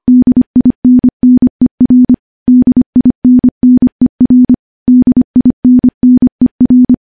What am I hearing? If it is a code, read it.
Morse code, heard as a keyed tone, about 25 wpm, 250 Hz. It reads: DINNER DINNER DINNER
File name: morse / morse-training-25WPM